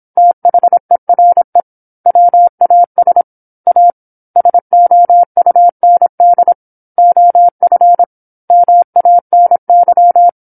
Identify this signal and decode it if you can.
Morse code, heard as a keyed tone, about 26 wpm, 700 Hz. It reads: THERE WAS A SOUND OF MANY
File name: 08-Chapter-08 26 WPM